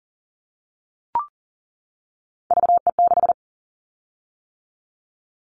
Morse code, heard as a keyed tone, about 42 wpm, 700 Hz. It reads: VE6
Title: Callsign Prefixes for Top 25 DXCC - 40wpm